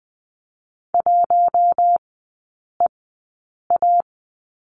Morse code, heard as a keyed tone, about 20 wpm, spaced as for 8 wpm, 700 Hz. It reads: 1EA
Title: 047 Lesson 9e - Number 4 - Three Characters 20wpm